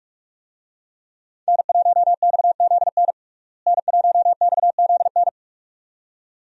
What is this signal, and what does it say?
Morse code, heard as a keyed tone, about 46 wpm, 700 Hz. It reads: N1XZN N1XZN
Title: USA Call Signs - 5 Characters - Repeated 2x - 45wpm